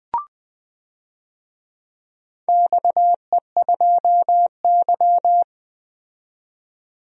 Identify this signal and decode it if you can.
Morse code, heard as a keyed tone, about 20 wpm, 700 Hz. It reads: XE2Y
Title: DXCC-100 Call Signs - 4 Characters - 20wpm